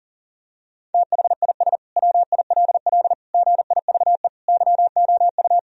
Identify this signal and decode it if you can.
Morse code, heard as a keyed tone, about 40 wpm, 700 Hz. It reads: THIS WILL GIVE YOU
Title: Sentences from Top 1000 Words - Review Part 3 - 40wpm